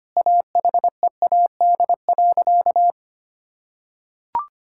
Morse code, heard as a keyed tone, about 25 wpm, 700 Hz. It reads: AHEAD.
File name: practice / 25wpm / Sentences with 2-Word Phrases - 25wpm